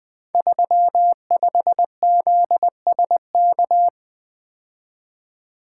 Morse code, heard as a keyed tone, about 20 wpm, 700 Hz. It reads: 35ZSK
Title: Code Groups - 5 Letter:Numbers - 20wpm